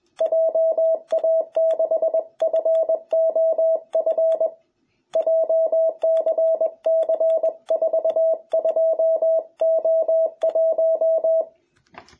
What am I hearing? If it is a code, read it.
Morse code, heard as a keyed tone, about 21 wpm, 660 Hz. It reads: JA6FOF JCC42O1
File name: jh8cbh / 2021cw